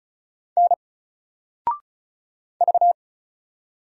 Morse code, heard as a keyed tone, about 36 wpm, 700 Hz. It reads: N V
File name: Single-Letter-Number-35wpm